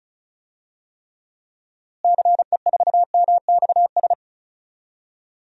Morse code, heard as a keyed tone, about 34 wpm, 700 Hz. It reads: CE4MXS
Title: DXCC-100 Call Signs - 6 Characters - Repeated 3x - 35wpm